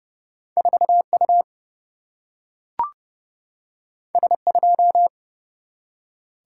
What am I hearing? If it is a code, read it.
Morse code, heard as a keyed tone, about 30 wpm, 700 Hz. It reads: 4U S2
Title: Single-Letter Single-Number 30wpm